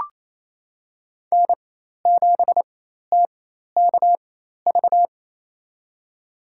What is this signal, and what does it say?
Morse code, heard as a keyed tone, about 28 wpm, spaced as for 13 wpm, 690 Hz. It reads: N7TKV
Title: USA Call Signs - 5 Characters - 4x Character Spacing - 28wpm